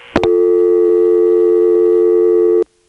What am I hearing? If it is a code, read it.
Dial tone (North America), 350 + 440 Hz.